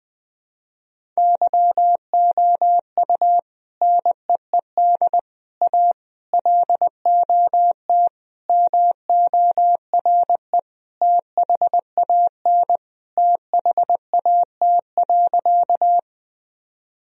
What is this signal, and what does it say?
Morse code, heard as a keyed tone, about 20 wpm, 700 Hz. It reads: YOU NEED A LOT MORE THAN THAT.
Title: Sentences from Top 1000 Words - Review Part 3 - 20wpm